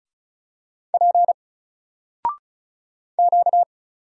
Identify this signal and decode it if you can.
Morse code, heard as a keyed tone, about 33 wpm, 700 Hz. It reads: P Q
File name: Single-Letter-Number-35wpm